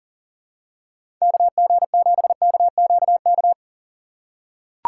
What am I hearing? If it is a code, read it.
Morse code, heard as a keyed tone, about 40 wpm, 700 Hz. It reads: KG7KQK
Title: USA Call Signs - 6 Characters - 3x Character Spacing - 40wpm